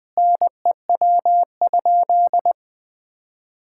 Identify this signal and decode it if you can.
Morse code, heard as a keyed tone, about 20 wpm, 700 Hz. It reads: NEW?
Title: Sentences with 6-Word Phrases - 20wpm